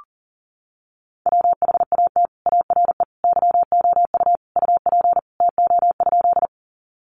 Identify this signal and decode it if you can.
Morse code, heard as a keyed tone, about 40 wpm, 700 Hz. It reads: WHAT ARE YOU UP TO?
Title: Sentences from Top 100 Words 40wpm